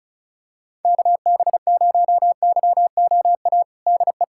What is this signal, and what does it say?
Morse code, heard as a keyed tone, about 34 wpm, 700 Hz. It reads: KB0YOA DE
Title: USA Callsign DE Callsign - 6 Characters - 35wpm